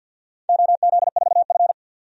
Morse code, heard as a keyed tone, about 51 wpm, 700 Hz. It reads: OZ4F
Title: DXCC-100 Call Signs - 4 Characters - 50wpm